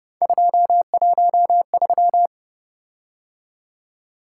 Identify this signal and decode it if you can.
Morse code, heard as a keyed tone, about 29 wpm, 700 Hz. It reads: 213
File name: Speed-Racing - Three Letter-Number Combinations - 25wpm